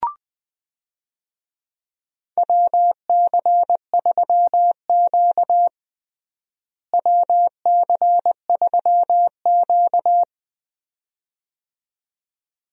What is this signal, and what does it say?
Morse code, heard as a keyed tone, about 21 wpm, 700 Hz. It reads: WC3Q WC3Q
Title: USA Call Signs - 4 Characters - Repeated 2x - 20wpm